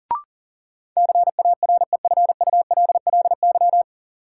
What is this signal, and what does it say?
Morse code, heard as a keyed tone, about 40 wpm, 700 Hz. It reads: CAREFULLY
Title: Sentences from Top 1300 Words - New Word Drill Alt - 40wpm